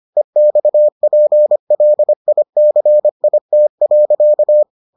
Morse code, heard as a keyed tone, about 25 wpm, 600 Hz. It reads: EXPLICIT.